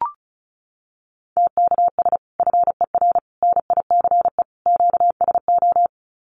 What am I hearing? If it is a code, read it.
Morse code, heard as a keyed tone, about 34 wpm, 700 Hz. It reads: TKS FER NICE QSO
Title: Common QSO Phrases 35wpm